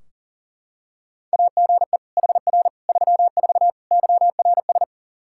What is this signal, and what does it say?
Morse code, heard as a keyed tone, about 40 wpm, 700 Hz. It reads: AGE HR 34 YRS
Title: Practice QSO - Exchange 3 - More Info - 40wpm